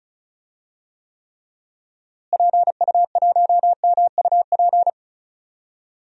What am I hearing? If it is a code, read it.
Morse code, heard as a keyed tone, about 36 wpm, 700 Hz. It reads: PU1MUP
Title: DXCC-100 Call Signs - 6 Characters - Repeated 2x - 35wpm